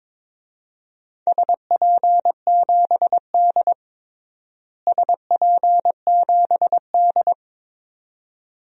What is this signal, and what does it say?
Morse code, heard as a keyed tone, about 22 wpm, 700 Hz. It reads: SP7D SP7D
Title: DXCC-100 Call Signs - 4 Characters - Repeated 2x - 22wpm